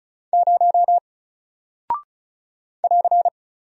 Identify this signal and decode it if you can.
Morse code, heard as a keyed tone, about 36 wpm, 700 Hz. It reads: 0 <AR>
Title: Single Character (Letter:Number:Punctuation:Prosign) 35wpm